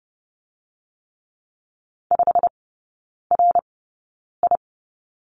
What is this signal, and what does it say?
Morse code, heard as a keyed tone, about 30 wpm, spaced as for 9 wpm, 700 Hz. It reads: 5RI
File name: course / 30:10wpm / 087 Lesson 15e - Number 5 - Three Characters 30wpm